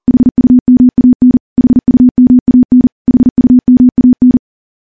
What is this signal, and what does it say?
Morse code, heard as a keyed tone, about 40 wpm, 250 Hz. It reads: HUMAN HUMAN HUMAN
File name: morse / morse-training-40WPM